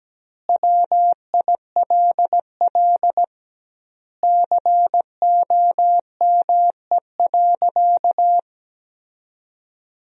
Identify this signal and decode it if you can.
Morse code, heard as a keyed tone, about 17 wpm, 700 Hz. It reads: WILL COME.
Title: 2x Word-Spacing - Sentences from Top 100 Words - 17wpm